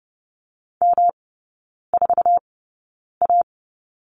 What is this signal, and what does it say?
Morse code, heard as a keyed tone, about 30 wpm, spaced as for 9 wpm, 700 Hz. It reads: M4A
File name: Lesson 18e - Letter m - Three Characters 30wpm